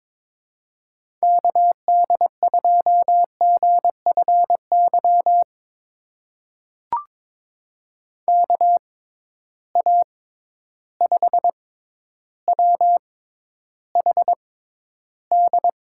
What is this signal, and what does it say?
Morse code, heard as a keyed tone, about 22 wpm, spaced as for 11 wpm, 700 Hz. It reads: KD2GFY KA5WHD
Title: USA Call Signs - 6 Characters - 6x Character Spacing - 22wpm